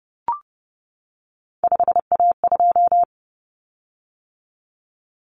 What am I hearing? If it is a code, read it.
Morse code, heard as a keyed tone, about 30 wpm, 700 Hz. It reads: 5A2
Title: Common Cut Numbers - 3-Digit 30wpm